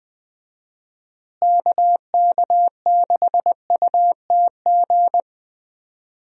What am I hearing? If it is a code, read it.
Morse code, heard as a keyed tone, about 20 wpm, 700 Hz. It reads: KK6UTG
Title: USA Call Signs - 6 Characters - Repeated 3x - 20wpm